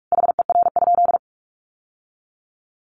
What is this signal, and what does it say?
Morse code, heard as a keyed tone, about 44 wpm, 700 Hz. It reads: HER?
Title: Sentences from Top 900 Words 45wpm